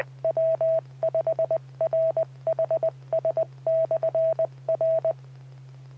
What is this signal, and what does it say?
Morse code, heard as a keyed tone, about 20 wpm, 650 Hz, with radio noise. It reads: W5RHS/R
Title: Saline, AK SKYWARN Repeater System - W5RHS